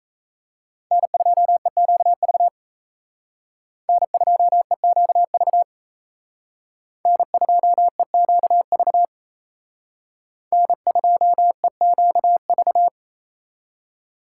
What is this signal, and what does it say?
Morse code, heard as a keyed tone, about 33 wpm, 700 Hz. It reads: N2EQV N2EQV N2EQV N2EQV
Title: Speed Racing - USA 1x3 Call Signs 28wpm